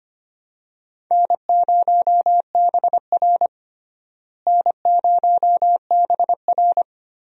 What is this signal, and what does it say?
Morse code, heard as a keyed tone, about 25 wpm, 700 Hz. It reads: N0BR N0BR
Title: USA Call Signs - 4 Characters - Repeated 3x - 25wpm